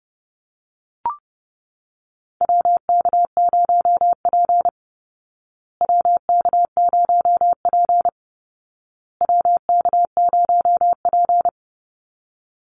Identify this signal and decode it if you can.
Morse code, heard as a keyed tone, about 30 wpm, 700 Hz. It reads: WK0P WK0P WK0P
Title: Call Signs Repeated 2x - 30wpm